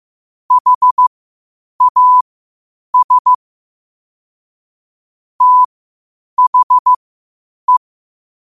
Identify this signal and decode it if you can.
Morse code, heard as a keyed tone, about 15 wpm, spaced as for 8 wpm, 1000 Hz. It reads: HAS THE
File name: qotdfiles / 8wpm